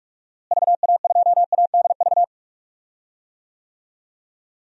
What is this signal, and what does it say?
Morse code, heard as a keyed tone, about 42 wpm, 700 Hz. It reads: VA2ADV
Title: DXCC-100 Call Signs - 6 Characters - Repeated 3x - 45wpm